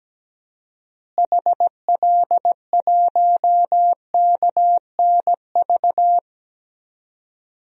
Morse code, heard as a keyed tone, about 17 wpm, 700 Hz. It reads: HL1KNV
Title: DXCC-100 Call Signs - 6 Characters - Repeated 3x - 17wpm